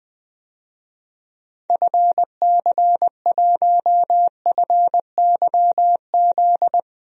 Morse code, heard as a keyed tone, about 21 wpm, 700 Hz. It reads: FC1FYZ
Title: DXCC-100 Call Signs - 6 Characters - 20wpm